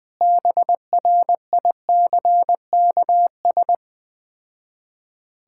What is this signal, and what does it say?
Morse code, heard as a keyed tone, about 20 wpm, 700 Hz. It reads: BRICKS
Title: Difficult Words with Hints - 20wpm